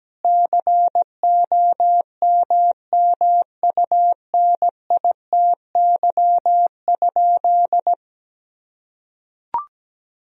Morse code, heard as a keyed tone, about 17 wpm, 700 Hz. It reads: COMMUNITY?
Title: Sentences from Top 1000 Words - Review Part 4 - 17wpm